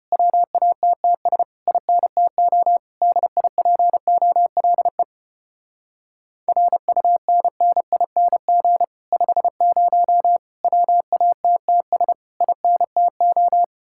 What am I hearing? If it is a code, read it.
Morse code, heard as a keyed tone, about 33 wpm, 700 Hz. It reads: WATTS INTO DIPOLE RUNNING 50 WATTS INTO